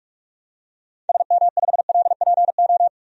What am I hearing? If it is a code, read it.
Morse code, heard as a keyed tone, about 46 wpm, 700 Hz. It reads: SM5LPO
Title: DXCC-100 Call Signs - 6 Characters - 45wpm